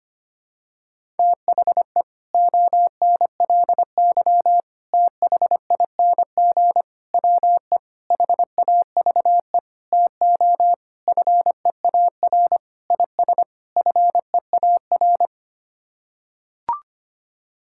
Morse code, heard as a keyed tone, about 25 wpm, 700 Hz. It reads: THE ONLY THING WE HAVE TO FEAR IS FEAR
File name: Sets of 9 Words - 25wpm